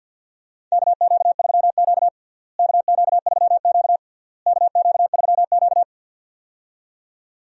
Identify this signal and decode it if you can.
Morse code, heard as a keyed tone, about 52 wpm, 700 Hz. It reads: KQ3Q KQ3Q KQ3Q
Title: USA Call Signs - 4 Characters - Repeated 3x - 50wpm